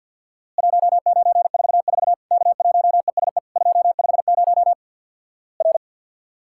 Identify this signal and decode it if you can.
Morse code, heard as a keyed tone, about 52 wpm, 700 Hz. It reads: 1944 K1ESE 250 R
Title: Worked All Europe DX Contest - QTC Exchange with Repeats - 50wpm